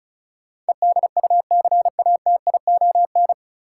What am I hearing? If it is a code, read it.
Morse code, heard as a keyed tone, about 36 wpm, 700 Hz. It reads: EDUCATION
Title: Sentences from Top 1000 Words - Review Part 6 - New Word Drill - 35wpm